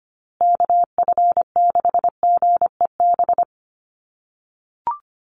Morse code, heard as a keyed tone, about 25 wpm, 700 Hz. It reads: KF6GEB
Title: USA Call Signs Repeated 3x 25wpm